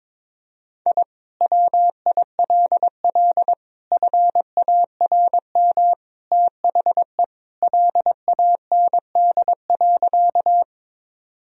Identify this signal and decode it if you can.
Morse code, heard as a keyed tone, about 22 wpm, 700 Hz. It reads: I WILL FARM THE LAND.
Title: Sentences from Top 1000 Words - Review Part 2 - 22wpm